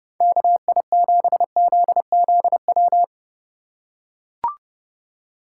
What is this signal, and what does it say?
Morse code, heard as a keyed tone, about 30 wpm, 700 Hz. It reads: KI7ZZW